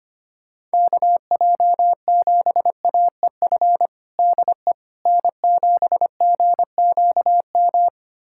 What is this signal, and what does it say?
Morse code, heard as a keyed tone, about 25 wpm, 700 Hz. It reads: KJ7AEF DE N7GQM
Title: USA Callsign DE Callsign 25wpm